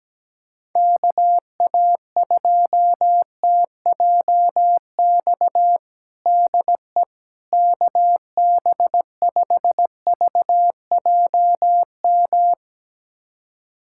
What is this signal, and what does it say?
Morse code, heard as a keyed tone, about 17 wpm, 700 Hz. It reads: KA2TJX DE KB5VJM
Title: USA Callsign DE Callsign 17wpm